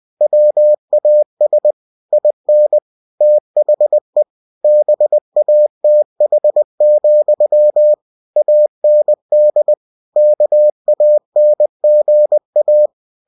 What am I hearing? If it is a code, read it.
Morse code, heard as a keyed tone, about 20 wpm, 600 Hz. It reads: WAS IN THE BATH, AND KANGA
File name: cw-books / winnie-the-pooh / winnie-20wpm-page-0072